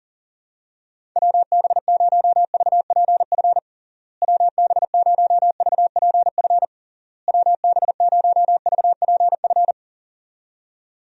Morse code, heard as a keyed tone, about 40 wpm, 700 Hz. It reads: WB0VPF WB0VPF WB0VPF
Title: USA Call Signs - 6 Characters - Repeated 3x - 40wpm